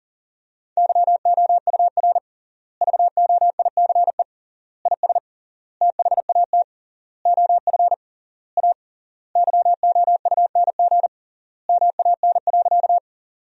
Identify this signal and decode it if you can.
Morse code, heard as a keyed tone, about 40 wpm, 700 Hz. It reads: YOUR VOICE IS THAT OF A YOUNG MAN.
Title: 3x Word-Spacing - Sentences from Top 500 Words - 40wpm